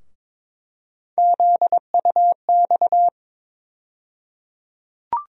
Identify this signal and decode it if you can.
Morse code, heard as a keyed tone, about 22 wpm, 700 Hz. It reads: ZUX
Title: USA Call Sign Postfixes - 3 Letters 22wpm